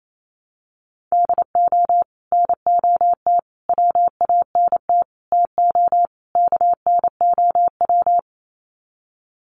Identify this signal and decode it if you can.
Morse code, heard as a keyed tone, about 28 wpm, 700 Hz. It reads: DO NOT WANT TO KNOW
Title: Sets of 5 Words Encore 28wpm